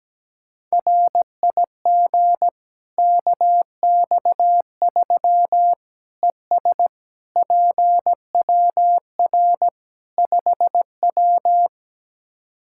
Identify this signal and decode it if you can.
Morse code, heard as a keyed tone, about 17 wpm, 700 Hz. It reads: RIG KX3 ES PWR 5W
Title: Practice QSO - Exchange 2 - Info - 17wpm